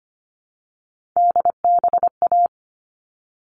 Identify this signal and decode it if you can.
Morse code, heard as a keyed tone, about 25 wpm, 700 Hz. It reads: DBA
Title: All Cut Numbers - 3-Digit 25wpm